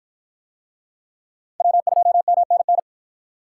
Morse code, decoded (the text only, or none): W2MNN